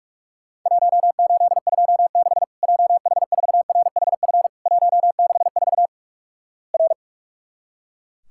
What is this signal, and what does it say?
Morse code, heard as a keyed tone, about 44 wpm, 700 Hz. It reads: 1826 JH4RHF 164 R
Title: Worked All Europe DX Contest - QTC Exchange - 45wpm